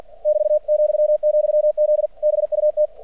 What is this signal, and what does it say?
Morse code, heard as a keyed tone, about 50 wpm, 600 Hz. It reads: <BT>,,Z BUT
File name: archived / CRSnet